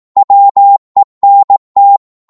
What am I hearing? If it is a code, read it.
Morse code, heard as a keyed tone, about 18 wpm, 800 Hz. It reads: WENT
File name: Morse / psg-18wpm